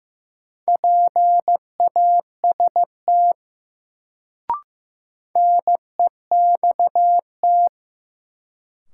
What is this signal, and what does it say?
Morse code, heard as a keyed tone, about 15 wpm, 700 Hz. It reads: PAST NEXT
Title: Sets of 2 Words - New Word Drill - 15wpm